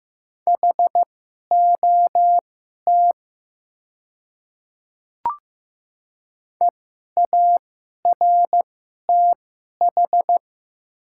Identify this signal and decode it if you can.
Morse code, heard as a keyed tone, about 15 wpm, spaced as for 11 wpm, 700 Hz. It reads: HOT EARTH
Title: 2x Character Spacing - Top 200 Words - 15wpm